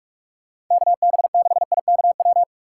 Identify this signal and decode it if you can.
Morse code, heard as a keyed tone, about 44 wpm, 700 Hz. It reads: KB6IKW